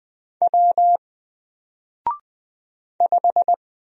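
Morse code, heard as a keyed tone, about 20 wpm, 700 Hz. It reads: W 5
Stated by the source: Single letter and number Morse drill at 20 wpm, from N4LCD.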